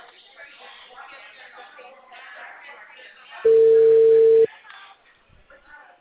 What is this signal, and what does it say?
Ringback tone (China, 450 Hz), with speech.